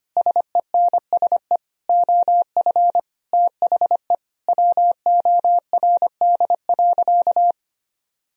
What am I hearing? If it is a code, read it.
Morse code, heard as a keyed tone, about 25 wpm, 700 Hz. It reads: SENSE OF THE WORD.
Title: Sentences from Top 1000 Words - Review Part 4 - 25wpm